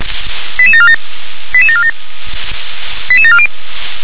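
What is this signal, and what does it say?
Five-tone selective call (ZVEI), heard as quick runs of five tones, tones of about 70 ms. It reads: 88747 88747 88740
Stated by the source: Five-tone ZVEI selective call 88915, received by SDR.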